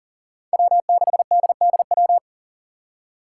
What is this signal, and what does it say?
Morse code, heard as a keyed tone, about 38 wpm, 700 Hz. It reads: W6DDW
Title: Five Challenging Letter-Number Combinations 40wpm